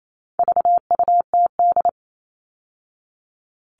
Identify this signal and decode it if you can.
Morse code, heard as a keyed tone, about 28 wpm, 700 Hz. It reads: VUTD
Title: All Cut Numbers - 4-Digit 28wpm